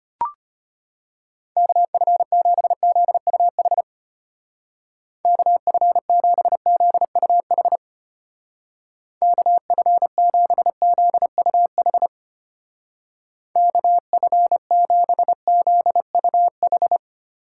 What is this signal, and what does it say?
Morse code, heard as a keyed tone, about 31 wpm, 700 Hz. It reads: KF7ZUH KF7ZUH KF7ZUH KF7ZUH
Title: Speed Racing - USA 2x3 Call Signs 25wpm